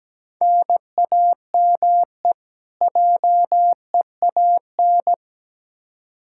Morse code, heard as a keyed tone, about 17 wpm, 700 Hz. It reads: NAME JEAN
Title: Practice QSO - Exchange 1 - Beginning - 17wpm